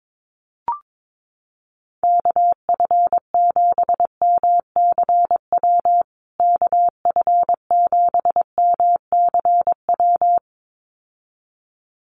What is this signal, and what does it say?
Morse code, heard as a keyed tone, about 22 wpm, 700 Hz. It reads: KF7MCW KF7MCW
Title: USA Call Signs Repeated 2x 22wpm